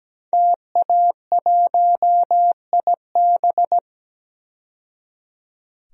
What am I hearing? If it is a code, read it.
Morse code, heard as a keyed tone, about 17 wpm, 700 Hz. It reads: TA1IB
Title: DXCC-100 Call Signs - 5 Characters - 17wpm